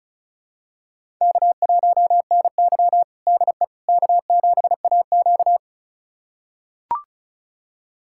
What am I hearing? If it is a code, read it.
Morse code, heard as a keyed tone, about 36 wpm, 700 Hz. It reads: K1NY DE K7AQ
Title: USA Callsign DE Callsign - 4 Characters - 35wpm